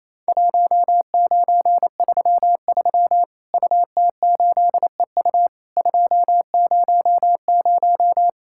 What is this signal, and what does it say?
Morse code, heard as a keyed tone, about 28 wpm, 700 Hz. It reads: 1933 UT8EU 200
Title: Worked All Europe DX Contest - QTC Exchange - 28wpm